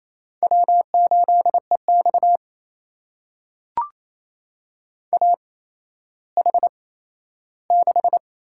Morse code, heard as a keyed tone, about 28 wpm, spaced as for 14 wpm, 700 Hz. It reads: W8EX AH6
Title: USA Call Signs - 4 Characters - 8x Character Spacing - 28wpm